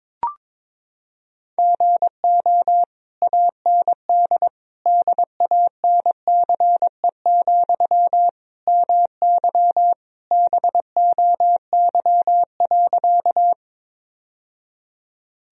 Morse code, heard as a keyed tone, about 22 wpm, 700 Hz. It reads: GO AND DANCE, MY BOY.
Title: Sentences from Top 1000 Words - Review Part 4 - 22wpm